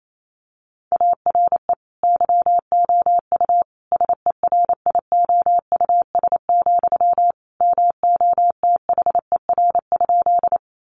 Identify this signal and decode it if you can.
Morse code, heard as a keyed tone, about 28 wpm, 700 Hz. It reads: ARE YOU SERIOUS, MOTHER?
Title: Sentences from Top 1100 Words 28wpm